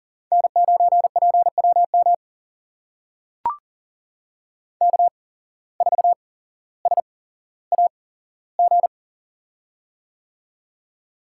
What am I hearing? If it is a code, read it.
Morse code, heard as a keyed tone, about 40 wpm, spaced as for 17 wpm, 700 Hz. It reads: N9PWM K4SAG
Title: USA Call Signs - 5 Characters - 8x Character Spacing - 40wpm